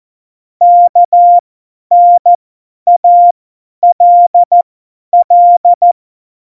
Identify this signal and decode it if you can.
Morse code, heard as a keyed tone, about 14 wpm, spaced as for 10 wpm, 700 Hz. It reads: KNALL
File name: Morsekurs / Witze / Witz 25. September 2018, 50 F70